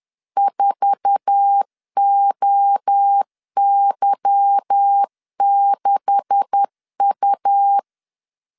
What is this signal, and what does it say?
Morse code, heard as a keyed tone, about 10 wpm, 800 Hz. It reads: 4OY6U
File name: CW / DET / 1 9 1500C S A 10wpm 4511